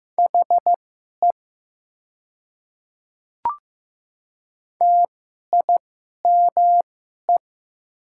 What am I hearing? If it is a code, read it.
Morse code, heard as a keyed tone, about 15 wpm, spaced as for 11 wpm, 700 Hz. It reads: HE TIME
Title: 2x Character Spacing - Top 100 Words - 15wpm